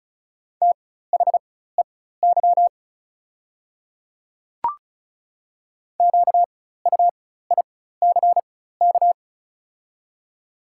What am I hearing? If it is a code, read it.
Morse code, heard as a keyed tone, about 36 wpm, spaced as for 16 wpm, 690 Hz. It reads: THEY QUICK
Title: 4x Character Spacing - Top 500 Words - 35wpm